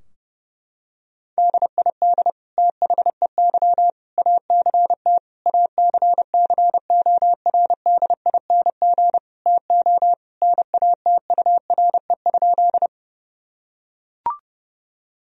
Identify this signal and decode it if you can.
Morse code, heard as a keyed tone, about 30 wpm, 700 Hz. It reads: DID THEY ACT ACCORDING TO NATURE?
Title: Sentences from Top 1000 Words - Review Part 4 - 30wpm